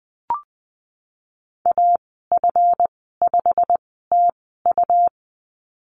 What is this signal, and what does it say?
Morse code, heard as a keyed tone, about 20 wpm, spaced as for 15 wpm, 700 Hz. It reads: AF5TU
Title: 147 Lesson 24g - Letter y - Call Signs 20wpm